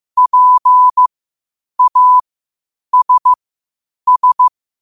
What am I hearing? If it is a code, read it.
Morse code, heard as a keyed tone, about 15 wpm, spaced as for 8 wpm, 1000 Hz. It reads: PASS
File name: qotdfiles / 8wpm